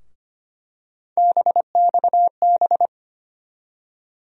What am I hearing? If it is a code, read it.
Morse code, heard as a keyed tone, about 25 wpm, 700 Hz. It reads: BXB
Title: USA Call Sign Postfixes - 3 Letters 25wpm